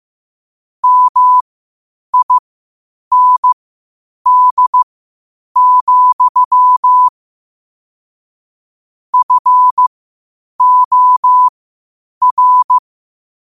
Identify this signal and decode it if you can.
Morse code, heard as a keyed tone, about 15 wpm, spaced as for 8 wpm, 1000 Hz. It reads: MIND, FOR
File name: qotdfiles / 8wpm